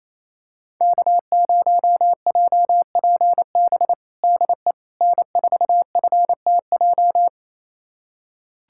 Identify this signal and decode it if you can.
Morse code, heard as a keyed tone, about 27 wpm, 700 Hz. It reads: K0JPB DE N4FTJ
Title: USA Callsign DE Callsign - 5 Characters - 28wpm